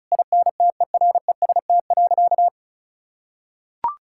Morse code, heard as a keyed tone, about 36 wpm, 700 Hz. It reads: INTEREST.
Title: Sentences from Top 1000 Words - Review Part 6 - 35wpm